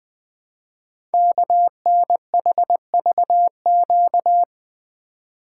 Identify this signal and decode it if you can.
Morse code, heard as a keyed tone, about 21 wpm, 700 Hz. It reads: KNHVQ
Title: Code Groups - 5 Letters - 20wpm